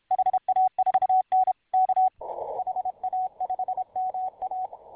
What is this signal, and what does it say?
Morse code, heard as a keyed tone, about 30 wpm, 750 Hz. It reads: HA4N K HA5MA